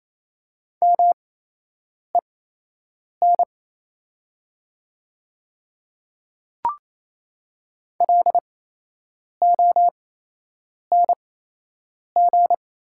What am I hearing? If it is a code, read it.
Morse code, heard as a keyed tone, about 28 wpm, spaced as for 8 wpm, 700 Hz. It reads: MEN LONG
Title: 8x Character Spacing - Top 500 Words - 28wpm